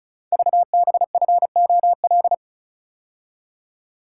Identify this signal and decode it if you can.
Morse code, heard as a keyed tone, about 36 wpm, 700 Hz. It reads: VBFOL
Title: Code Groups - 5 Letters - 35wpm